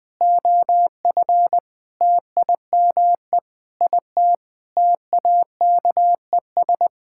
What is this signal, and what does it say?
Morse code, heard as a keyed tone, about 20 wpm, 700 Hz. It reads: OF TIME IT TAKES
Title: Sentences with 7-Word Phrases - 20wpm